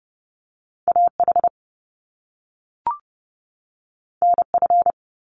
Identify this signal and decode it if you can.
Morse code, heard as a keyed tone, about 30 wpm, 700 Hz. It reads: AH NF
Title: Top Two Letter Combinations in Words 30wpm